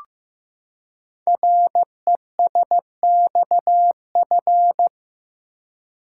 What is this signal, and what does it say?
Morse code, heard as a keyed tone, about 15 wpm, 700 Hz. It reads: RESXF
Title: Code Groups - 5 Letters - 15wpm